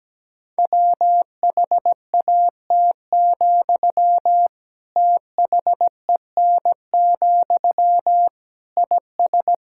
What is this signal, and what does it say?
Morse code, heard as a keyed tone, about 17 wpm, 700 Hz. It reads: WHAT, THEN, IS
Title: Sentences from Top 1000 Words - Review Part 6 - 17wpm